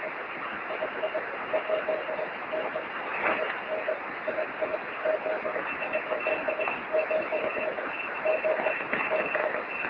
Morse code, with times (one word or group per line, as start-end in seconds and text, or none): none